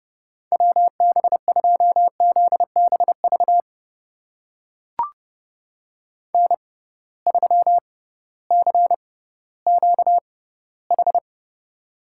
Morse code, heard as a keyed tone, about 32 wpm, spaced as for 17 wpm, 700 Hz. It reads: WB2ZBV N3CQH
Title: USA Call Signs - 6x Character Spacing - 30wpm